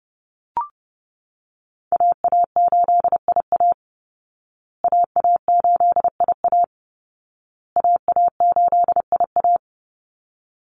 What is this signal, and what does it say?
Morse code, heard as a keyed tone, about 30 wpm, 700 Hz. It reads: AA8IA AA8IA AA8IA
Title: Call Signs Repeated 2x - 30wpm